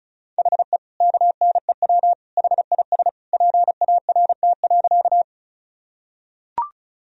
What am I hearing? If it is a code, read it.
Morse code, heard as a keyed tone, about 36 wpm, 700 Hz. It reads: HE KNEW HIS PART.
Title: Sentences from Top 1000 Words - Review Part 1 - 35wpm